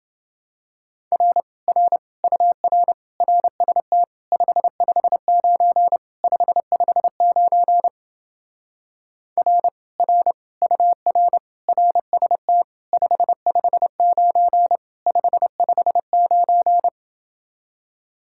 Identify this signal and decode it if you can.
Morse code, heard as a keyed tone, about 29 wpm, 700 Hz. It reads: R R UR RST 559 559 R R UR RST 559 559